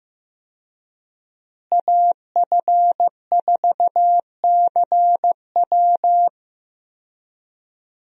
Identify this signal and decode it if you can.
Morse code, heard as a keyed tone, about 15 wpm, 700 Hz. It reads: AF4CW
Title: USA Call Signs - 5 Characters - Repeated 2x - 15wpm